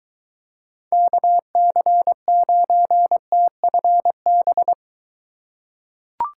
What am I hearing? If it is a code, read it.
Morse code, heard as a keyed tone, about 20 wpm, 700 Hz. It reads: KC9TFB